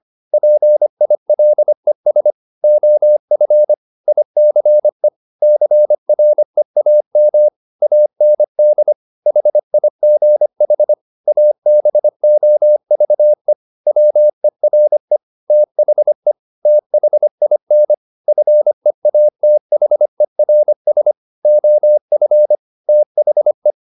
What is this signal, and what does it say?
Morse code, heard as a keyed tone, about 25 wpm, 600 Hz. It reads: PILES OF ICE CREAM AND HIGH ABOVE WERE THE THIN FEATHERS OF THE
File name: the old man and the sea part 1 25wpm